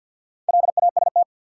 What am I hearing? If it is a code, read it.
Morse code, heard as a keyed tone, about 51 wpm, 700 Hz. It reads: LAST
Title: Sets of 2 Words - New Word Drill - 50wpm